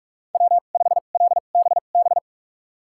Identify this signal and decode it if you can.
Morse code, heard as a keyed tone, about 46 wpm, spaced as for 32 wpm, 700 Hz. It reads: W5LBB
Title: USA Call Signs - 5 Characters - 2x Character Spacing - 45wpm